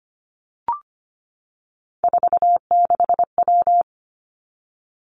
Morse code, heard as a keyed tone, about 25 wpm, 700 Hz. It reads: 46W
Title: Three Challenging Letter-Number Combinations 25wpm